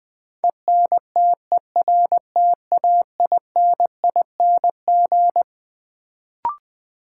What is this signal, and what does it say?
Morse code, heard as a keyed tone, about 20 wpm, 700 Hz. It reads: ENTERTAINING
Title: Build A Word - 6 Letters At A Time - 12-Letter Words - 20wpm